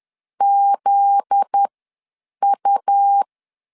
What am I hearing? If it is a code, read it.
Morse code, heard as a keyed tone, about 10 wpm, spaced as for 7 wpm, 800 Hz. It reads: ZU
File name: LEZIONI / 7 GF9Z / 7 GF9Z 10wpm 2611